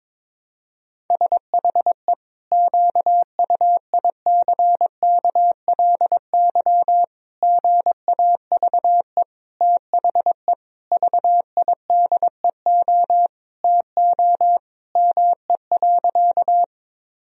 Morse code, heard as a keyed tone, about 22 wpm, 700 Hz. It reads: SHE QUICKLY GAVE THE VIDEO TO ME.
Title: Sentences from Top 1000 Words - Review Part 4 - 22wpm